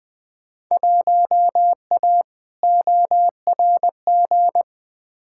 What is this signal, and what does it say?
Morse code, heard as a keyed tone, about 20 wpm, 700 Hz. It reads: 1A ORG
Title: Field Day Exchanges Encore - 20wpm